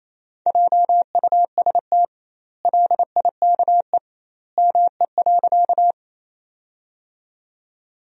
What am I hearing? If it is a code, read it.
Morse code, heard as a keyed tone, about 28 wpm, 700 Hz. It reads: JUST LIKE ME.
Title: 2x Word-Spacing - Sentences from Top 200 Words - 28wpm